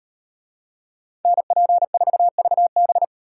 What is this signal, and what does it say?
Morse code, heard as a keyed tone, about 38 wpm, 700 Hz. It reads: NP4VB